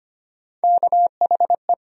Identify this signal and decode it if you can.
Morse code, heard as a keyed tone, about 25 wpm, 700 Hz. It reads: KHE
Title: USA Call Sign Postfixes - 3 Letters 25wpm